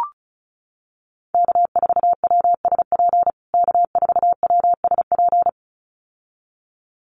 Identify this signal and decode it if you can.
Morse code, heard as a keyed tone, about 34 wpm, 700 Hz. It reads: K4WSP K4WSP
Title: USA Call Signs Repeated 2x 35wpm